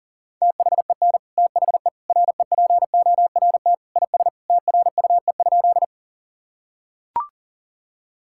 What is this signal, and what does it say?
Morse code, heard as a keyed tone, about 40 wpm, 700 Hz. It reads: THEN THE REPORT IS TRUE?
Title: Sentences from Top 1000 Words - Review Part 6 - 40wpm